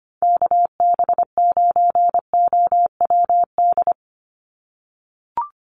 Morse code, heard as a keyed tone, about 25 wpm, 700 Hz. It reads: KB9OWD
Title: Call Signs - 25wpm